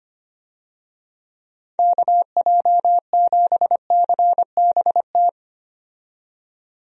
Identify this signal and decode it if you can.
Morse code, heard as a keyed tone, about 25 wpm, 700 Hz. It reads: KJ7CBT
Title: USA Call Signs - 6 Characters - 25wpm